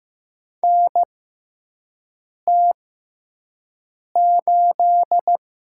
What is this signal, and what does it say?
Morse code, heard as a keyed tone, about 15 wpm, spaced as for 5 wpm, 700 Hz. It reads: NT8
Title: USA Call Signs - 4 Characters - 6x Character Spacing - 15wpm